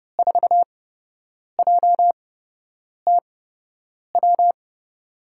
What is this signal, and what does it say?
Morse code, heard as a keyed tone, about 32 wpm, spaced as for 8 wpm, 700 Hz. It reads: 4JTW
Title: USA Call Signs - 5 Characters - 8x Character Spacing - 30wpm